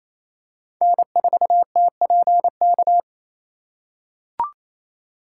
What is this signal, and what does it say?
Morse code, heard as a keyed tone, about 27 wpm, 700 Hz. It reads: N4TPK
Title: USA Call Signs - 4x Character Spacing - 28wpm